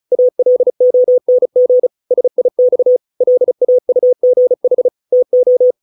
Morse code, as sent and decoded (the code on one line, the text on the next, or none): .- .-.. --- -. --. / ... .. -..- / .-.. .- ..- --. .... / - ---
ALONG SIX LAUGH TO